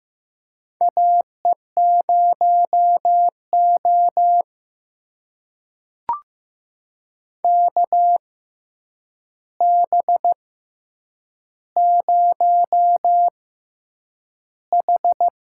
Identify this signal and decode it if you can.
Morse code, heard as a keyed tone, about 15 wpm, spaced as for 8 wpm, 700 Hz. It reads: AE0O KB0H
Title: USA Call Signs - 4 Characters - 6x Character Spacing - 15wpm